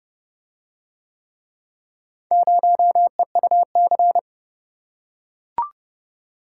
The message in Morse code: 0EUC

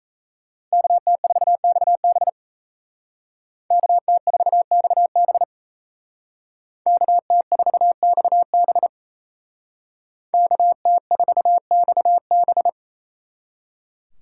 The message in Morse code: KT4XB KT4XB KT4XB KT4XB

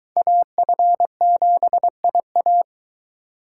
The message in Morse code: AF7IA

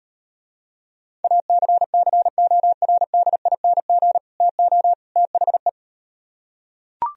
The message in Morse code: ACCORDING TO THE

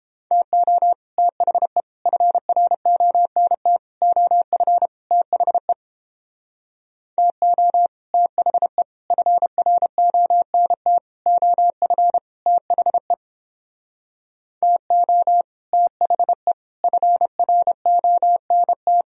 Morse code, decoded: TO THE FRONT OF THE TO THE FRONT OF THE TO THE FRONT

